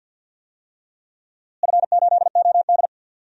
Morse code, heard as a keyed tone, about 52 wpm, 700 Hz. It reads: F8OD
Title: DXCC-100 Call Signs - 4 Characters - Repeated 3x - 50wpm